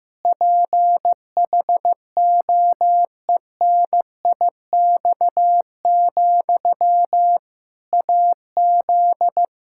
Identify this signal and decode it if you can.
Morse code, heard as a keyed tone, about 16 wpm, 700 Hz. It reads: PHOENIX, AZ
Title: Practice QSO - Exchange 1 - Beginning - 15wpm